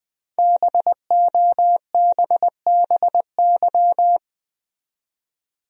Morse code, heard as a keyed tone, about 20 wpm, 700 Hz. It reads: BOBBY
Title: USA Common Names - 20wpm